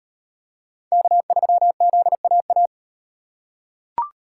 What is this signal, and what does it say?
Morse code, heard as a keyed tone, about 36 wpm, 700 Hz. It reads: K3ZAA